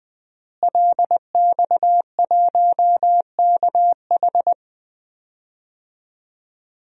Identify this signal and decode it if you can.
Morse code, heard as a keyed tone, about 20 wpm, 700 Hz. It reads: LX1KH
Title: DXCC-100 Call Signs - 5 Characters - Repeated 3x - 20wpm